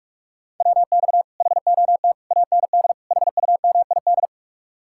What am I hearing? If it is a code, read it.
Morse code, heard as a keyed tone, about 62 wpm, spaced as for 53 wpm, 700 Hz. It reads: WX HOT AND HUMID
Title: Speed-Racing Common QSO Phrases - 45wpm